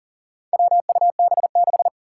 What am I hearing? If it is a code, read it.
Morse code, heard as a keyed tone, about 40 wpm, 700 Hz. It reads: WUB6
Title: Four Challenging Letter-Number Combinations 40wpm